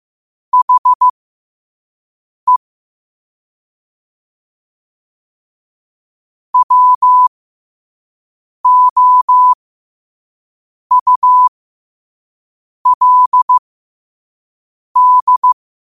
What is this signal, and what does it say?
Morse code, heard as a keyed tone, about 15 wpm, spaced as for 5 wpm, 1000 Hz. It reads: HE WOULD
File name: qotdfiles / 5wpm